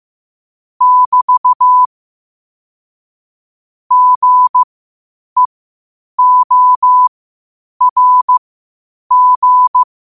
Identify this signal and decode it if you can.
Morse code, heard as a keyed tone, about 15 wpm, spaced as for 8 wpm, 1000 Hz. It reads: <BT> GEORG
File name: qotdfiles / 8wpm